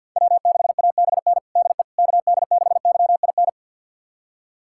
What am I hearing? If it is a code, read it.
Morse code, heard as a keyed tone, about 52 wpm, 700 Hz. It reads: W6ABN DE KD6YIN
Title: USA Callsign DE Callsign 50wpm